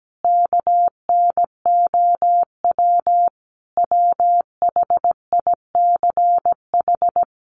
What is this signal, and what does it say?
Morse code, heard as a keyed tone, about 17 wpm, 700 Hz. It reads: KNOW WHICH
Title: A Sound Of Thunder 17wpm